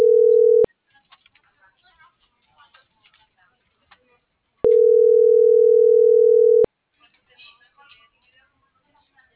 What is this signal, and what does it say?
Ringback tone (North America), 440 + 480 Hz.